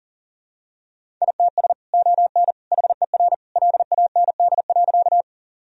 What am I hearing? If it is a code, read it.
Morse code, heard as a keyed tone, about 40 wpm, 700 Hz. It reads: ITS ON HER LAND.
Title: Sentences from Top 1000 Words - Review Part 1 - 40wpm